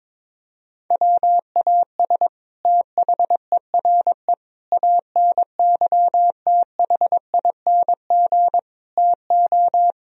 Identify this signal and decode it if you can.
Morse code, heard as a keyed tone, about 21 wpm, 700 Hz. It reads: WAS THERE ANYTHING TO
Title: Sentences from Top 1000 Words - Review Part 1 - 22wpm